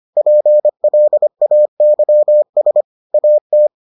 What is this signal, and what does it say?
Morse code, heard as a keyed tone, about 25 wpm, 600 Hz. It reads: PLAYS AT